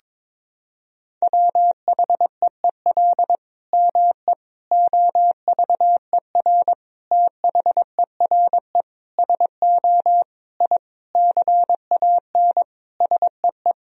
Morse code, heard as a keyed tone, about 22 wpm, 700 Hz. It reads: WHEEL ME OVER THERE SO I CAN SEE
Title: Sentences from Top 1000 Words - Review Part 3 - 22wpm